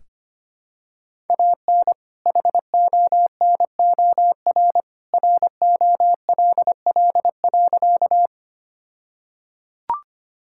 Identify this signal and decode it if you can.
Morse code, heard as a keyed tone, about 25 wpm, 700 Hz. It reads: AN HONOR ROLL.
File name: practice / 25wpm / All-Age Appropriate Jokes - 25wpm